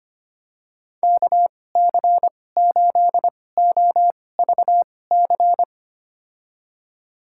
Morse code, heard as a keyed tone, about 25 wpm, spaced as for 18 wpm, 700 Hz. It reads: KC8OVC
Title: USA Call Signs - 6 Characters - 2x Character Spacing - 25wpm